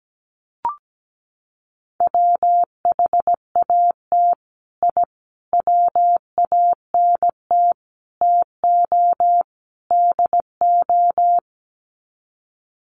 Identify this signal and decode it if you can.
Morse code, heard as a keyed tone, about 17 wpm, 700 Hz. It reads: WHAT I WANT TO DO
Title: Sets of 5 Words Encore 17wpm